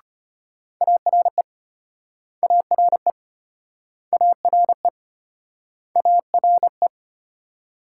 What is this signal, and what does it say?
Morse code, heard as a keyed tone, about 35 wpm, 700 Hz. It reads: ARE ARE ARE ARE